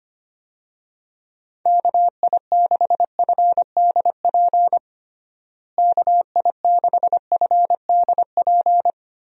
Morse code, heard as a keyed tone, about 25 wpm, 700 Hz. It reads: KI6FDP KI6FDP
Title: USA Call Signs - 6 Characters - Repeated 3x - 25wpm